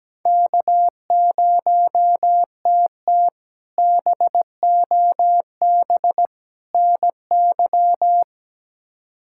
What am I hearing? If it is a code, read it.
Morse code, heard as a keyed tone, about 17 wpm, 700 Hz. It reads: K0TT BOB NY
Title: K1USN SST QSO Practice - 17wpm